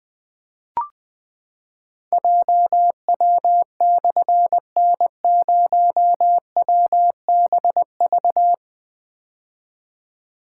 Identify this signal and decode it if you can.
Morse code, heard as a keyed tone, about 20 wpm, 700 Hz. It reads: JW/N0WBV
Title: Difficult Call Signs 20wpm